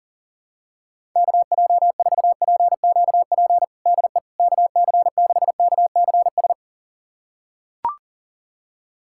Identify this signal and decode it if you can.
Morse code, heard as a keyed tone, about 40 wpm, 700 Hz. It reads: KJ4PQP DE KC6KCS
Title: USA Callsign DE Callsign - 6 Characters - 40wpm